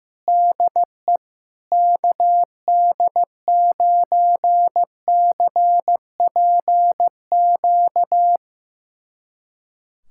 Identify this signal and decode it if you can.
Morse code, heard as a keyed tone, about 15 wpm, 700 Hz. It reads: DE KD9CPQ
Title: USA Callsign DE Callsign 15wpm